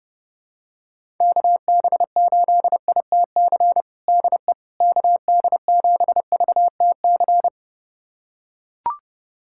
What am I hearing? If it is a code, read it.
Morse code, heard as a keyed tone, about 32 wpm, 700 Hz. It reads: KB8ITC DE KD7VTC
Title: USA Callsign DE Callsign - 6 Characters - 30wpm